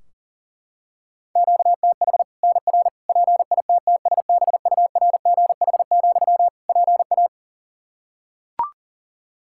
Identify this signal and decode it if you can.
Morse code, heard as a keyed tone, about 40 wpm, 700 Hz. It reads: QTH NR PITTSBURGH, PA
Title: Practice QSO - Exchange 1 - Beginning - 40wpm